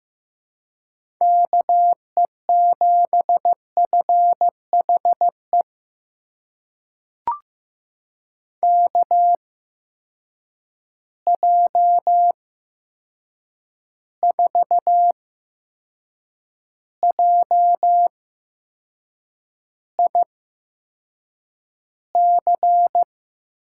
Morse code, heard as a keyed tone, about 15 wpm, spaced as for 6 wpm, 700 Hz. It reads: KE7FHE KJ4JIC
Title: USA Call Signs - 6 Characters - 8x Character Spacing - 15wpm